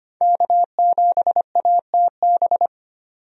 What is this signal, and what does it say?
Morse code, heard as a keyed tone, about 25 wpm, 700 Hz. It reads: K7ATB
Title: USA Call Signs - 5 Characters - 25wpm